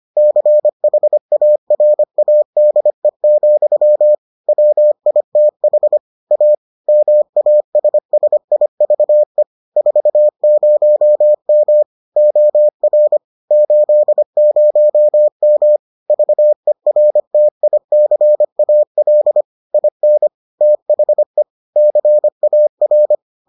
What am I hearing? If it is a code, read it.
Morse code, heard as a keyed tone, about 25 wpm, 600 Hz. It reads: CHARADE, WITH A MASSIVE 40M OR 80M VERTICAL IN THE CAR